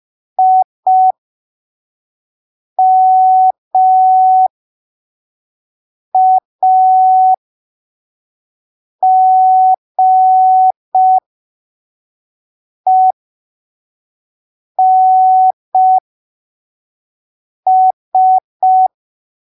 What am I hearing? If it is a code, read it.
Morse code, heard as a keyed tone, about 5 wpm, spaced as for 3 wpm, 750 Hz. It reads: IMAGENS
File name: arquivos / texto / texto 18 05ppm